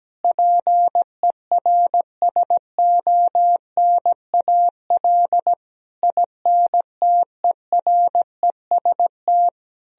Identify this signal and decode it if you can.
Morse code, heard as a keyed tone, about 17 wpm, 700 Hz. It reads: PERSONAL INTEREST
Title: Sentences from Top 1000 Words - Review Part 4 - 17wpm